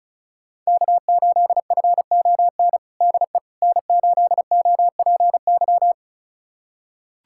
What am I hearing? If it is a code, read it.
Morse code, heard as a keyed tone, about 36 wpm, 700 Hz. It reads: K8FON DE N8OPY
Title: USA Callsign DE Callsign - 5 Characters - 35wpm